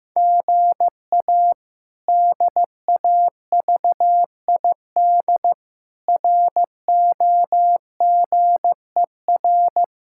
Morse code, heard as a keyed tone, about 15 wpm, 700 Hz. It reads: GA DAVID ROGER
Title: K1USN SST QSO Practice - 15wpm